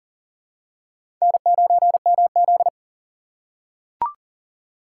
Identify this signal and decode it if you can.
Morse code, heard as a keyed tone, about 40 wpm, 700 Hz. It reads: N9MZ
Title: USA Call Signs - 2x Character Spacing - 40wpm